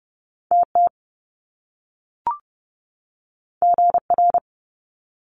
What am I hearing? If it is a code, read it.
Morse code, heard as a keyed tone, about 30 wpm, 700 Hz. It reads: TT GR
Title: Top Two Letter Combinations in Words 30wpm